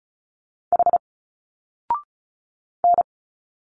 Morse code, heard as a keyed tone, about 34 wpm, 700 Hz. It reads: H N
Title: Single Letters 35wpm